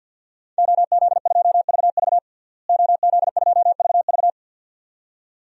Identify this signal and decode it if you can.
Morse code, heard as a keyed tone, about 51 wpm, 700 Hz. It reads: OZ2VV OZ2VV
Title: DXCC-100 Call Signs - 5 Characters - Repeated 2x - 50wpm